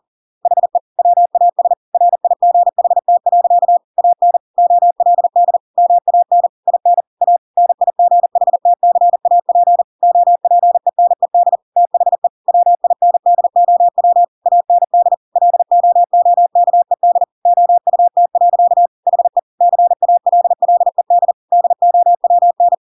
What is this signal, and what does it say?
Morse code, heard as a keyed tone, about 40 wpm, 700 Hz. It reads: HE WAS RIGHT. AN OLD MAN IN A NIGHTCAP OPENED THE WINDOW AND LOOKED OUT. HE CALLED DOWN
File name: Pinocchio / Pinocchio ch6 40 40 ik1hge